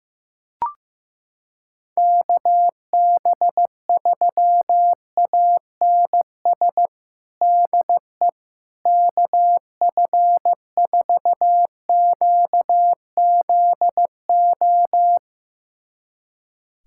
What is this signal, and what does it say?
Morse code, heard as a keyed tone, about 15 wpm, 700 Hz. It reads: KB3ANS DE KF4QZO
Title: USA Callsign DE Callsign - 6 Characters - 15wpm